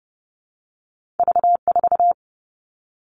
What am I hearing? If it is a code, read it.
Morse code, heard as a keyed tone, about 30 wpm, 700 Hz. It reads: V4
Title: Two Challenging Letter-Number Combinations 30wpm